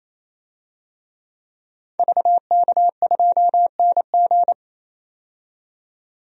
Morse code, heard as a keyed tone, about 28 wpm, 700 Hz. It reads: VK2NG
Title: DXCC-100 Call Signs - 5 Characters - 28wpm